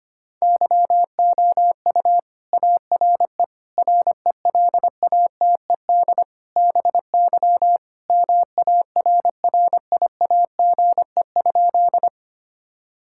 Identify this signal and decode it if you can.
Morse code, heard as a keyed tone, about 25 wpm, 700 Hz. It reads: YOU ARE RELATED BY MARRIAGE?
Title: Sentences from Top 1000 Words - Review Part 5 - 25wpm